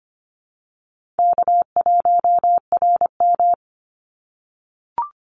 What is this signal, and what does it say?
Morse code, heard as a keyed tone, about 25 wpm, 700 Hz. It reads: K1RM
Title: Easy Call Signs - 25wpm